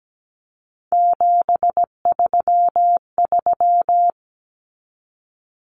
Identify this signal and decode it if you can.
Morse code, heard as a keyed tone, about 17 wpm, 700 Hz. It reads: 733
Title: Common Cut Numbers - 3-Digit 17wpm